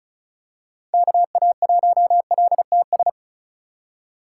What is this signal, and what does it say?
Morse code, heard as a keyed tone, about 36 wpm, 700 Hz. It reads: KA1LTS
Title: USA Call Signs - 3x Character Spacing - 35wpm